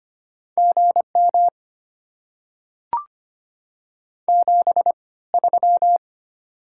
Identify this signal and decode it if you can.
Morse code, heard as a keyed tone, about 25 wpm, spaced as for 18 wpm, 690 Hz. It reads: GM 73
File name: Top 90 QSO Elements - 3x Character Spacing - 25wpm